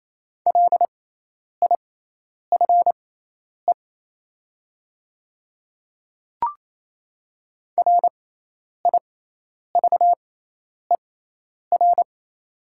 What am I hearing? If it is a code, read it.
Morse code, heard as a keyed tone, about 28 wpm, spaced as for 10 wpm, 700 Hz. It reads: LIFE RIVER
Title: 6x Character Spacing - Top 500 Words - 28wpm